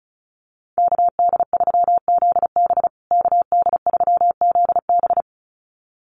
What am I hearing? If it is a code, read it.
Morse code, heard as a keyed tone, about 34 wpm, 700 Hz. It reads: KD3ZB KD3ZB
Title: USA Call Signs Repeated 2x 35wpm